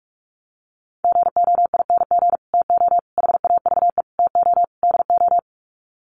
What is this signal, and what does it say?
Morse code, heard as a keyed tone, about 44 wpm, 700 Hz. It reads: GOING TO HAVE TO DO
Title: Sets of 5 Words Encore 45wpm